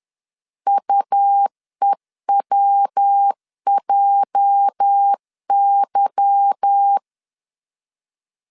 Morse code, heard as a keyed tone, about 10 wpm, 800 Hz. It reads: UEWJY
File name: LEZIONI / 8 YJ45 / 8 YJ 10wpm 2651